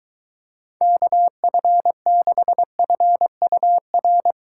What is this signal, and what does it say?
Morse code, heard as a keyed tone, about 23 wpm, 700 Hz. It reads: KF6FUR